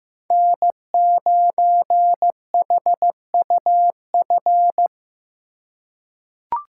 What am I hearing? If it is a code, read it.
Morse code, heard as a keyed tone, about 15 wpm, 700 Hz. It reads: N9HUF
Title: USA Call Signs - 5 Characters - 4x Character Spacing - 15wpm